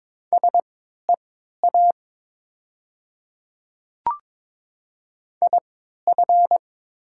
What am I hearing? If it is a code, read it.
Morse code, heard as a keyed tone, about 22 wpm, spaced as for 12 wpm, 700 Hz. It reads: SEA IF
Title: 3x Character Spacing - Top 300 Words - 22wpm